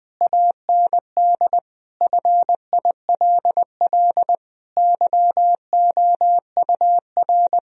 Morse code, heard as a keyed tone, about 19 wpm, 700 Hz. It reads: AND FILL YOUR
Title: Sentences from Top 1000 Words - Review Part 3 - 20wpm